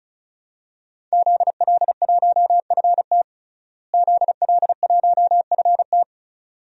Morse code, heard as a keyed tone, about 34 wpm, 700 Hz. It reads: ZL1FT ZL1FT
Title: DXCC-100 Call Signs - 5 Characters - Repeated 2x - 35wpm